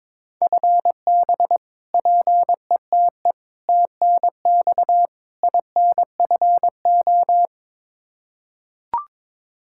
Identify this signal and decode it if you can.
Morse code, heard as a keyed tone, about 21 wpm, 700 Hz. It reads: FB PETE TNX INFO
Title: Practice QSO - Exchange 3 - More Info - 22wpm